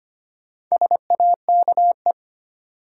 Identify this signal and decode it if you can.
Morse code, heard as a keyed tone, about 25 wpm, 700 Hz. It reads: SAKE
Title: Sets of 4 Words - New Word Drill - 25wpm